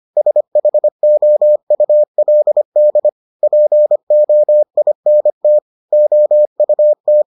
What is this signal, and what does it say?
Morse code, heard as a keyed tone, about 25 wpm, 600 Hz. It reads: SHOULD POINT OUT